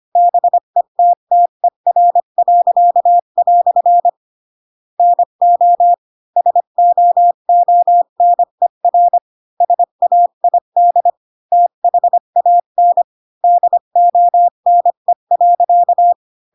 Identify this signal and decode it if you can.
Morse code, heard as a keyed tone, about 25 wpm, 700 Hz. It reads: BETTER." NO SOONER SAID THAN DONE.
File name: Pinocchio / Pinocchio ch5 20 25 ik1hge